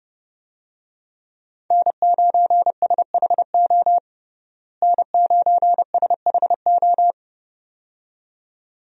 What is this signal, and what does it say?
Morse code, heard as a keyed tone, about 32 wpm, 700 Hz. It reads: N9SHO N9SHO
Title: USA Call Signs - 5 Characters - Repeated 2x - 30wpm